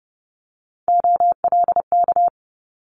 Morse code, heard as a keyed tone, about 30 wpm, 700 Hz. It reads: OLK
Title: Multi-letter Phonemes Quick Fire 30wpm